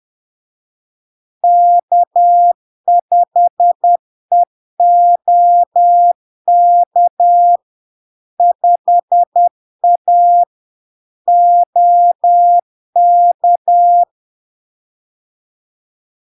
Morse code, heard as a keyed tone, about 10 wpm, 700 Hz. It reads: K5EOK 5A OK
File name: LARK-FD-2022-Log-in-CW-10wpm-700-Hz